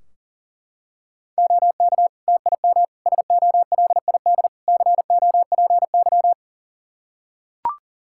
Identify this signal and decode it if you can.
Morse code, heard as a keyed tone, about 37 wpm, 700 Hz. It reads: OK TIM SOLID COPY
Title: Practice QSO - Exchange 3 - More Info - 40wpm